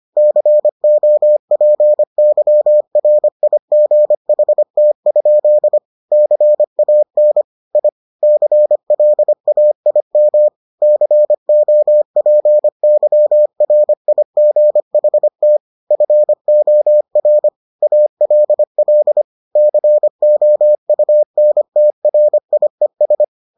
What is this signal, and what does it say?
Morse code, heard as a keyed tone, about 25 wpm, 600 Hz. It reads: COPYRIGHT? CAN I CLAIM COPYRIGHT FOR ALL COUNTRIES